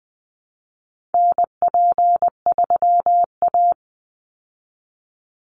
Morse code, heard as a keyed tone, about 20 wpm, 700 Hz. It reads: NP3A
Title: Call Signs Repeated 3x - 20wpm